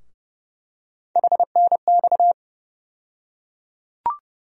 Morse code, heard as a keyed tone, about 30 wpm, 700 Hz. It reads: HNX
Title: USA Call Sign Postfixes - 3 Letters 30wpm